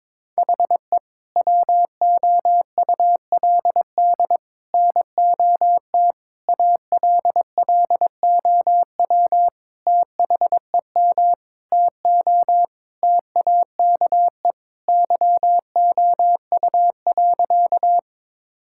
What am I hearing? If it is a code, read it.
Morse code, heard as a keyed tone, about 21 wpm, 700 Hz. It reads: HE WOULD NOT ALLOW THEM TO TAKE YOU.
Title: Sentences from Top 1000 Words - Review Part 5 - 22wpm